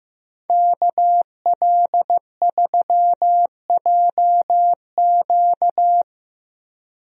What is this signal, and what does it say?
Morse code, heard as a keyed tone, about 15 wpm, 700 Hz. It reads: KL3JQ
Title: USA Call Signs - 5 Characters - Repeated 2x - 15wpm